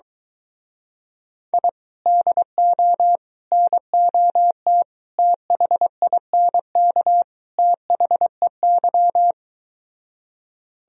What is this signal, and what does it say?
Morse code, heard as a keyed tone, about 23 wpm, 700 Hz. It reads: I DO NOT THINK THEY